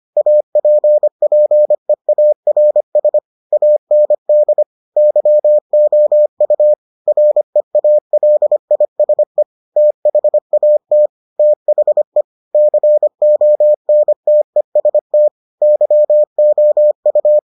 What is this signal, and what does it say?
Morse code, heard as a keyed tone, about 25 wpm, 600 Hz. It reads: APPEARS AND YOU REALISE THAT THE CONTEST YOU